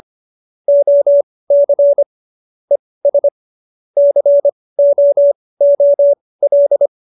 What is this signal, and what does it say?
Morse code, heard as a keyed tone, about 25 wpm, spaced as for 18 wpm, 570 Hz. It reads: OC ES COOL